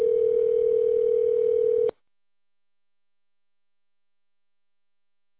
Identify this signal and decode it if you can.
Ringback tone (North America), 440 + 480 Hz.